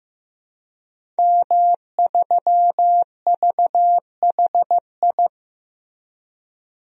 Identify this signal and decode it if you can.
Morse code, heard as a keyed tone, about 15 wpm, 700 Hz. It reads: M3VHI
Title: DXCC-100 Call Signs - 5 Characters - Repeated 2x - 15wpm